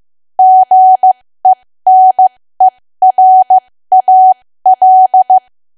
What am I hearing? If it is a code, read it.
Morse code, heard as a keyed tone, about 15 wpm, spaced as for 13 wpm, 750 Hz. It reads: GENERAL
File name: cwa / W1AW-Level2 / 13WPM